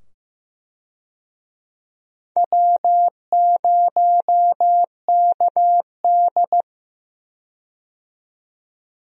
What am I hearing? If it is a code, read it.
Morse code, heard as a keyed tone, about 15 wpm, 700 Hz. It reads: W0KD
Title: USA Call Signs - 4 Characters - 15wpm